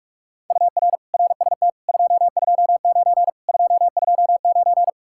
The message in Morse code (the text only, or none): UR RST 229 229